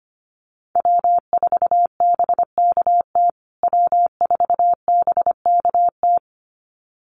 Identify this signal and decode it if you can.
Morse code, heard as a keyed tone, about 25 wpm, 700 Hz. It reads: W4BKT W4BKT
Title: USA Call Signs Repeated 2x 25wpm